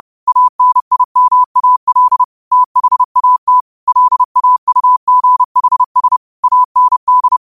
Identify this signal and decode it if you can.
Morse code, heard as a keyed tone, about 30 wpm, 1000 Hz. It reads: ANIMAL THAT LAUGHS AND